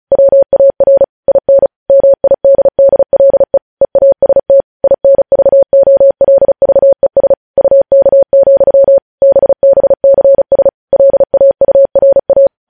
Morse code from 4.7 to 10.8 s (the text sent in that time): INVOLVES UK, BBCS